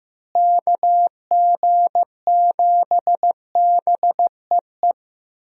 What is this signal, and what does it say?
Morse code, heard as a keyed tone, about 15 wpm, 700 Hz. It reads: KG7BEE
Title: USA Call Signs - 6 Characters - Repeated 3x - 15wpm